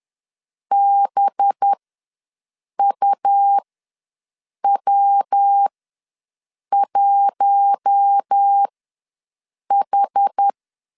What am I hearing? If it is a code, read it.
Morse code, heard as a keyed tone, about 10 wpm, spaced as for 6 wpm, 800 Hz. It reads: BUW1H
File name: LEZIONI / 5 LBVC / 5 LB ws 10wpm 3616